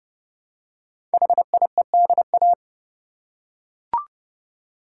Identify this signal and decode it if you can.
Morse code, heard as a keyed tone, about 30 wpm, 700 Hz. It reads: HIEDA
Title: Code Groups - 5 Letters - 30wpm